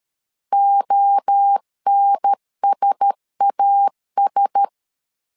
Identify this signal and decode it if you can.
Morse code, heard as a keyed tone, about 13 wpm, 800 Hz. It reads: ONSAS
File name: LEZIONI / 4 Q1S 12wpm 1802